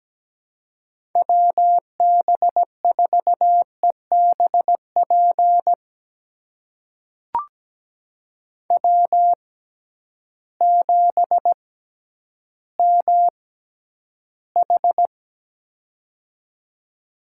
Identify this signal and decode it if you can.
Morse code, heard as a keyed tone, about 17 wpm, spaced as for 10 wpm, 700 Hz. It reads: WB4EBP W7MH
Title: USA Call Signs - 6x Character Spacing - 17wpm